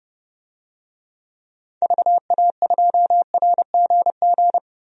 Morse code, heard as a keyed tone, about 30 wpm, 700 Hz. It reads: VA2RGG
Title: DXCC-100 Call Signs - 6 Characters - Repeated 2x - 30wpm